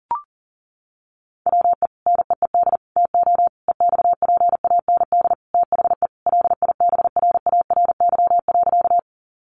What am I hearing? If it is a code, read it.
Morse code, heard as a keyed tone, about 43 wpm, 700 Hz. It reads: WE NEED TO EXPAND THE LIBRARY.
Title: Sentences from Top 1400 Words 40wpm